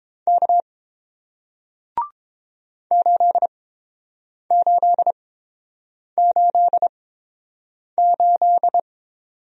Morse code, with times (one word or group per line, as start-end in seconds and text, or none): 0.3-0.6 K
2.9-8.8 8888